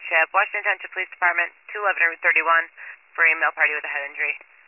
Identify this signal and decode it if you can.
Emergency Alert System (SAME) tones, heard as harsh buzzing bursts, with speech. It is heard from 0.9 to 3.7 s.